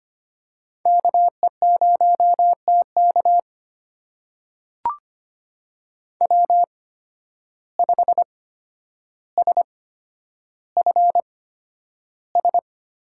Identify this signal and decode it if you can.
Morse code, heard as a keyed tone, about 25 wpm, spaced as for 11 wpm, 700 Hz. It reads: KE0TK W5SFS
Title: USA Call Signs - 5 Characters - 8x Character Spacing - 25wpm